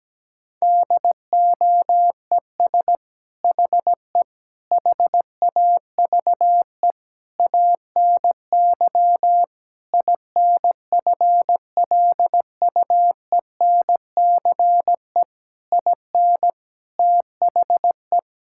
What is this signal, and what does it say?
Morse code, heard as a keyed tone, about 17 wpm, 700 Hz. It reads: DOES HE HAVE ANY INFLUENCE IN THE
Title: Sentences from Top 1600 Words 17wpm